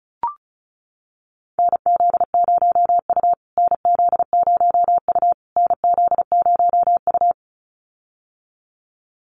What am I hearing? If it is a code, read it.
Morse code, heard as a keyed tone, about 34 wpm, 700 Hz. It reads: NZ0U NZ0U NZ0U
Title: USA Call Signs Repeated 3x 35wpm